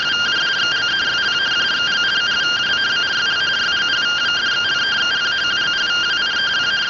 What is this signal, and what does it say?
Two-tone teleprinter signal, about 170 Hz shift.